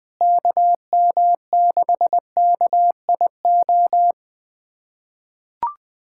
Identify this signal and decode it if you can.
Morse code, heard as a keyed tone, about 20 wpm, 700 Hz. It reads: KM6KIO
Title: USA Callsign DE Callsign 20wpm